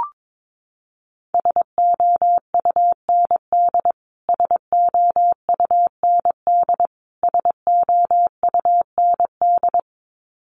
Morse code, heard as a keyed tone, about 22 wpm, 700 Hz. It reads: SOUND SOUND SOUND
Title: Top 200 Words Repeated 22wpm